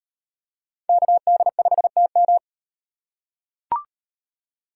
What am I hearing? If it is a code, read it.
Morse code, heard as a keyed tone, about 38 wpm, 700 Hz. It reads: KD5TM